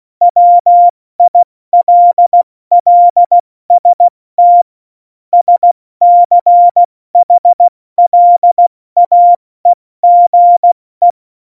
Morse code, heard as a keyed tone, about 16 wpm, 700 Hz. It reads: WILLST SCHLAEGE
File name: Morsekurs / Witze / Witz 28. August 2018, 70 F80